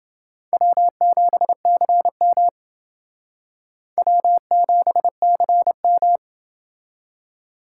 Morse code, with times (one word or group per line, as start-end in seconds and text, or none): 0.5-2.5 W7CM
4.0-6.2 W7CM